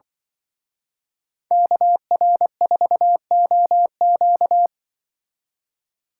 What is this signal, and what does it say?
Morse code, heard as a keyed tone, about 25 wpm, 700 Hz. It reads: KR4OQ
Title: Speed Racing - USA 2x2 Call Signs 20wpm